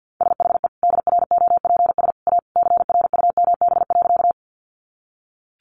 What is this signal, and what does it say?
Morse code, heard as a keyed tone, about 50 wpm, 700 Hz. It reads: SHE DROPS A CRUMB.
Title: Easy Phrases 50wpm